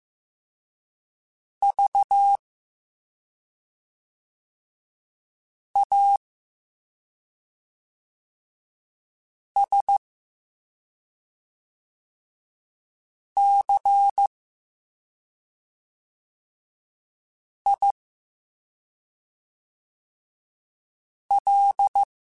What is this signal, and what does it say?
Morse code, heard as a keyed tone, about 15 wpm, spaced as for 2 wpm, 800 Hz. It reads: VASCIL